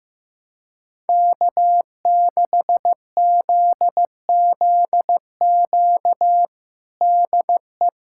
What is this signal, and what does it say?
Morse code, heard as a keyed tone, about 16 wpm, 700 Hz. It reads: K6ZZQ DE
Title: USA Callsign DE Callsign 15wpm